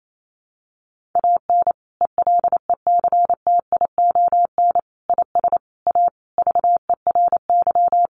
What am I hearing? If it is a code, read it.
Morse code, heard as a keyed tone, about 28 wpm, 700 Hz. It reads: AN ELECTION IS A VERY
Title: Sentences from Top 1500 Words 28wpm